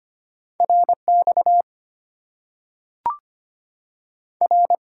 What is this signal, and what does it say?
Morse code, heard as a keyed tone, about 25 wpm, 700 Hz. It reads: RX R
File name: Top 90 QSO Elements - 8x Character Spacing - 25wpm